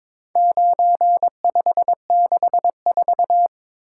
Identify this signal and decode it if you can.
Morse code, heard as a keyed tone, about 22 wpm, 700 Hz. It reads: 9564
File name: practice / 22wpm / Four Letter-Number Combinations - 22wpm